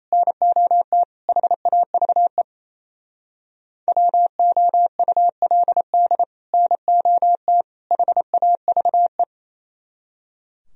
Morse code, heard as a keyed tone, about 36 wpm, 700 Hz. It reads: NOT HAVE WOULD NOT HAVE